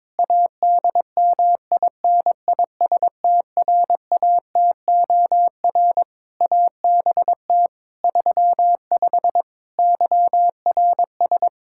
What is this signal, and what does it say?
Morse code, heard as a keyed tone, about 22 wpm, 700 Hz. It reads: ADMINISTRATOR ABT 35 YRS
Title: Practice QSO - Exchange 3 - More Info - 22wpm